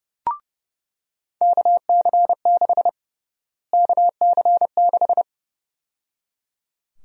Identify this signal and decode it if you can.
Morse code, heard as a keyed tone, about 30 wpm, 700 Hz. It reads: KC6 KC6
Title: USA Call Sign Prefixes - Repeated 2x - 30wpm